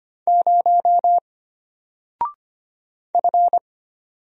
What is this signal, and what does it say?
Morse code, heard as a keyed tone, about 25 wpm, 700 Hz. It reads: 0 F
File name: Single-Letter-Number-25wpm